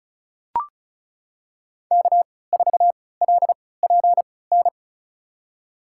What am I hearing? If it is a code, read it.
Morse code, heard as a keyed tone, about 36 wpm, spaced as for 20 wpm, 700 Hz. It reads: K4LPN
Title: USA Call Signs - 5 Characters - 3x Character Spacing - 35wpm